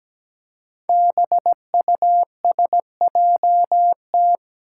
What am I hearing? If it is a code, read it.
Morse code, heard as a keyed tone, about 17 wpm, 700 Hz. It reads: BUSJT
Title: Five Challenging Letter-Number Combinations 17wpm